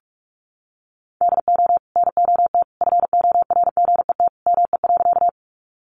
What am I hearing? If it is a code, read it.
Morse code, heard as a keyed tone, about 44 wpm, 700 Hz. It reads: DO NOT FORGET ME.
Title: Sentences from Top 1200 Words 45wpm